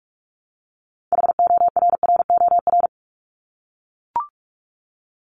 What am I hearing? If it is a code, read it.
Morse code, heard as a keyed tone, about 44 wpm, 700 Hz. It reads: HORROR
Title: Top 1501-1750 Words 45wpm